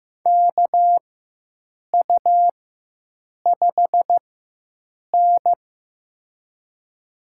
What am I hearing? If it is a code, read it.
Morse code, heard as a keyed tone, about 15 wpm, spaced as for 7 wpm, 700 Hz. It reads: KU5N
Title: USA Call Signs - 4x Character Spacing - 15wpm